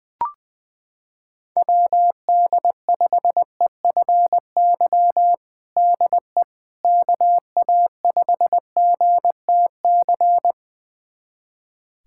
Morse code, heard as a keyed tone, about 21 wpm, 700 Hz. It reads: WD5EFY DE KA5GTC
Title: USA Callsign DE Callsign - 6 Characters - 20wpm